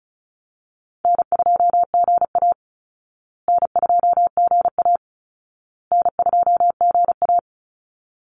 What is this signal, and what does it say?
Morse code, heard as a keyed tone, about 34 wpm, 700 Hz. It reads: N2GA N2GA N2GA
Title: Easy Call Signs Repeated 3x - 35wpm